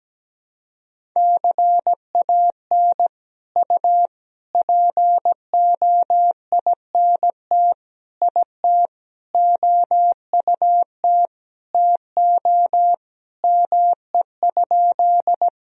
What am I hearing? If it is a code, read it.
Morse code, heard as a keyed tone, about 17 wpm, 700 Hz. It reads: CAN U POINT IT OUT TO ME?
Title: Sentences from Top 200 Words and Abbreviations 17wpm